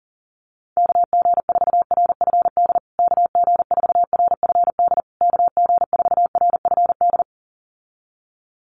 Morse code, heard as a keyed tone, about 40 wpm, 700 Hz. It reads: KG4RFD KG4RFD KG4RFD
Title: USA Call Signs Repeated 3x 40wpm